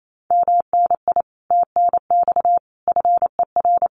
Morse code, heard as a keyed tone, about 28 wpm, 700 Hz. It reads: MNI TNX FER
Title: Common QSO Phrases Encore 28wpm